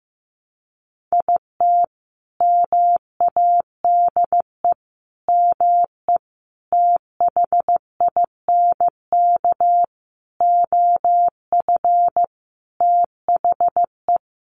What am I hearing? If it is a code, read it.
Morse code, heard as a keyed tone, about 14 wpm, 700 Hz. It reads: IT MADE ME THINK OF THE
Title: Sentences from Top 1100 Words 15wpm